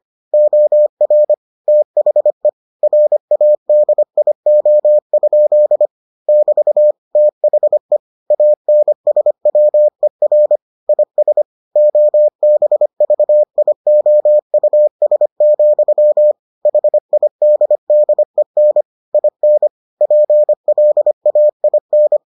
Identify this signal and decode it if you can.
Morse code, heard as a keyed tone, about 25 wpm, 600 Hz. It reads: OR THE RADIO? <BT> THE ANSWER IS OBVIOUS, HIDDEN IN PLAIN